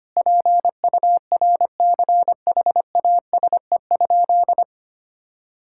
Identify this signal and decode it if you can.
Morse code, heard as a keyed tone, about 25 wpm, 700 Hz. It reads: PURCHASE?
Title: Sentences from Top 1000 Words - Review Part 6 - 25wpm